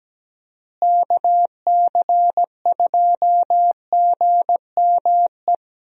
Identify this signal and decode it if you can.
Morse code, heard as a keyed tone, about 17 wpm, 700 Hz. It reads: KC2GME
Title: USA Call Signs - 6 Characters - Repeated 2x - 17wpm